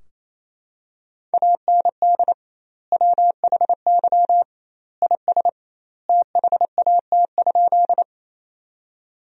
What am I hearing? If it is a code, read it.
Morse code, heard as a keyed tone, about 28 wpm, 700 Hz. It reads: AND WHY IS THAT?
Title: 2x Word-Spacing - Sentences from Top 200 Words - 28wpm